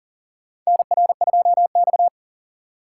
Morse code, heard as a keyed tone, about 40 wpm, 700 Hz. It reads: NR2X
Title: USA Call Signs - 4 Characters - 4x Character Spacing - 40wpm